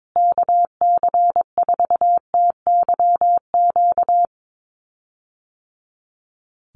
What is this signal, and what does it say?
Morse code, heard as a keyed tone, about 22 wpm, 700 Hz. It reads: KC4TYQ
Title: USA Call Signs Repeated 3x 22wpm